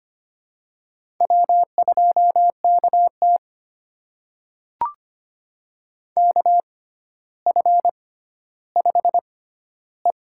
Morse code, heard as a keyed tone, about 25 wpm, spaced as for 13 wpm, 700 Hz. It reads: W2KT KF5E
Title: USA Call Signs - 4 Characters - 6x Character Spacing - 25wpm